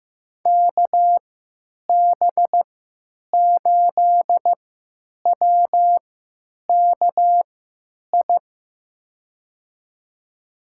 Morse code, heard as a keyed tone, about 15 wpm, spaced as for 9 wpm, 700 Hz. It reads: KB8WKI
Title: USA Call Signs - 3x Character Spacing - 15wpm